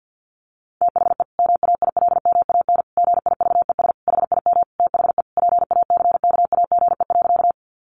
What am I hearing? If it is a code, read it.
Morse code, heard as a keyed tone, about 50 wpm, 700 Hz. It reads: THE MAILMAN GIVES HIM THE PACKAGE.
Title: Additional Easy Phrases 50wpm